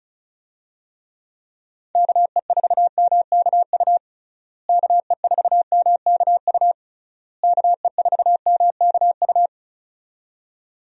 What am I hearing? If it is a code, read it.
Morse code, heard as a keyed tone, about 34 wpm, 700 Hz. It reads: KE4MKU KE4MKU KE4MKU
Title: USA Call Signs - 6 Characters - Repeated 3x - 35wpm